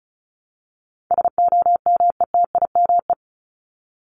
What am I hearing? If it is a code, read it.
Morse code, heard as a keyed tone, about 34 wpm, 700 Hz. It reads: SOMETIME
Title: ES:S Suffix - Top 100 Words with Base Words 35wpm